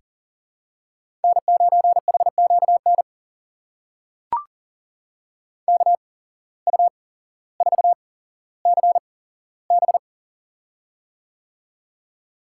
Morse code, heard as a keyed tone, about 40 wpm, spaced as for 17 wpm, 700 Hz. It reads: N9HQN KU4CB
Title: USA Call Signs - 5 Characters - 8x Character Spacing - 40wpm